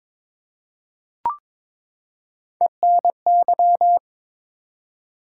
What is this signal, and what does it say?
Morse code, heard as a keyed tone, about 22 wpm, 700 Hz. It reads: ENY
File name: ARRL Sections v2 - 22wpm